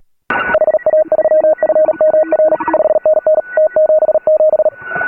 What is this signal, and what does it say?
Morse code, heard as a keyed tone, about 38 wpm, 590 Hz. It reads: HA3FMR 5NN T77
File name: AUDIO2018 / HA3FMR CW